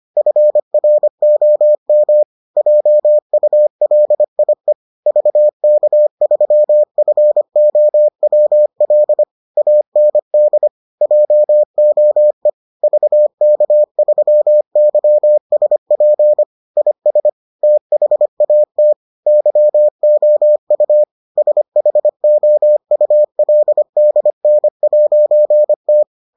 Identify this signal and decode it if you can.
Morse code, heard as a keyed tone, about 25 wpm, 600 Hz. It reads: FROM JULIE VK3FOWL AND JOE VK3YSP IS THAT YOU SHOULDN'T